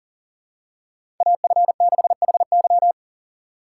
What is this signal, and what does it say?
Morse code, heard as a keyed tone, about 40 wpm, 700 Hz. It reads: AF6HY
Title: USA Call Signs - 5 Characters - 6x Character Spacing - 40wpm